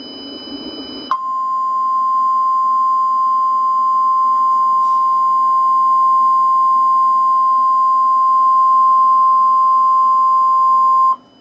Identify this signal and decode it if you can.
Emergency Alert System (SAME) tones (the 1050 Hz attention tone). It is heard from 1.1 to 11.2 s.
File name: ORIGINAL-Emergency-Alert-System-Tornado-Warning-for-Knoxville-TN-March-2-2012